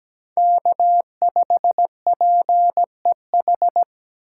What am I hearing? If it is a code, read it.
Morse code, heard as a keyed tone, about 17 wpm, 700 Hz. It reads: K5PEH
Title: USA Call Signs - 5 Characters - 2x Character Spacing - 17wpm